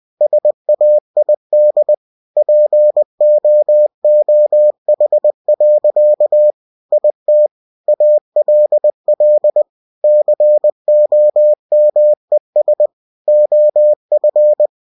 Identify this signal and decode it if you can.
Morse code, heard as a keyed tone, about 20 wpm, 600 Hz. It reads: SAID POOH. IT ALL COMES OF
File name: cw-books / winnie-the-pooh / winnie-20wpm-page-0039